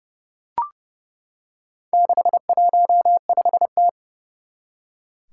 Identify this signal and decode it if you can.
Morse code, heard as a keyed tone, about 28 wpm, 700 Hz. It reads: 615T
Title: Four Challenging Letter-Number Combinations 30wpm